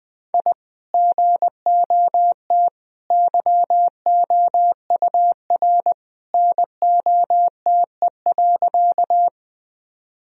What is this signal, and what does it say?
Morse code, heard as a keyed tone, about 20 wpm, 700 Hz. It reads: I GOT YOUR NOTE.
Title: Sentences from Top 1000 Words - Review Part 2 - 20wpm